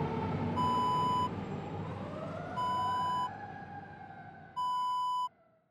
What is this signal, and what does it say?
There is no code.